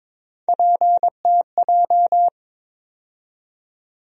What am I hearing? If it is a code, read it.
Morse code, heard as a keyed tone, about 22 wpm, 700 Hz. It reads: PTJ
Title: USA Call Sign Postfixes - 3 Letters 22wpm